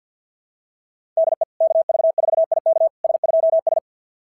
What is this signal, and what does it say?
Morse code, heard as a keyed tone, about 50 wpm, 690 Hz. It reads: DE KV4IK S2S
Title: SOTA QSO Practice 50wpm